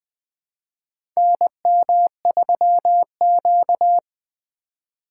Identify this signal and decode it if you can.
Morse code, heard as a keyed tone, about 20 wpm, 700 Hz. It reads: NM3Q
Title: USA Call Signs - 4 Characters - 2x Character Spacing - 20wpm